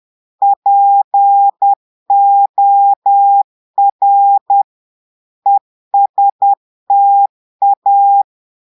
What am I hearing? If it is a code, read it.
Morse code, heard as a keyed tone, about 10 wpm, 800 Hz. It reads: POR ESTA
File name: Portugais - texte en clair 1 - vitesse - 10 mots minute